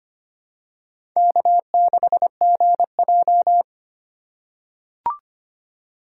Morse code, heard as a keyed tone, about 25 wpm, 700 Hz. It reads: K6GJ
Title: USA Call Signs - 2x Character Spacing - 25wpm